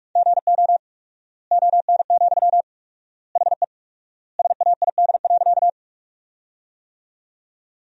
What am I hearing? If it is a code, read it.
Morse code, heard as a keyed tone, about 46 wpm, 700 Hz. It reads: GO ON, HE SAID.
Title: 4x Word-Spacing - Sentences from Top 100 Words - 45wpm